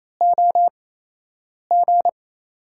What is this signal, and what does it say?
Morse code, heard as a keyed tone, about 28 wpm, spaced as for 8 wpm, 700 Hz. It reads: OG
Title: USA Call Signs - 5 Characters - 8x Character Spacing - 28wpm